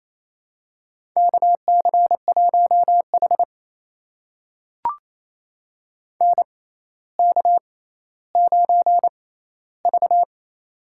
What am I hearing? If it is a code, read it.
Morse code, heard as a keyed tone, about 27 wpm, spaced as for 14 wpm, 700 Hz. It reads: KC1H NK9V
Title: USA Call Signs - 4 Characters - 6x Character Spacing - 28wpm